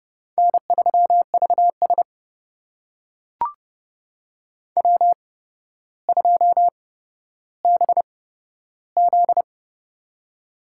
Morse code, heard as a keyed tone, about 32 wpm, spaced as for 13 wpm, 700 Hz. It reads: N3VS W2BZ
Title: USA Call Signs - 4 Characters - 8x Character Spacing - 30wpm